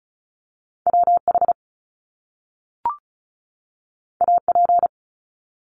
Morse code, heard as a keyed tone, about 36 wpm, 700 Hz. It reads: WH AP
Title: Top Two Letter Combinations in Words 35wpm